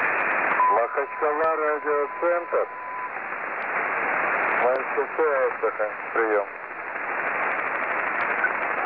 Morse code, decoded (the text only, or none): none